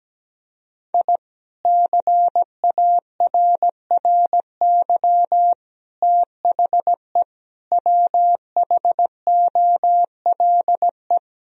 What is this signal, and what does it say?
Morse code, heard as a keyed tone, about 17 wpm, 700 Hz. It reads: I CARRY THE WHOLE
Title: Sentences from Top 1000 Words - Review Part 1 - 17wpm